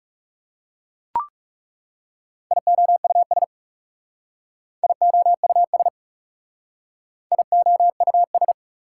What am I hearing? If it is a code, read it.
Morse code, heard as a keyed tone, about 40 wpm, 690 Hz. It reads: IOUS IOUS IOUS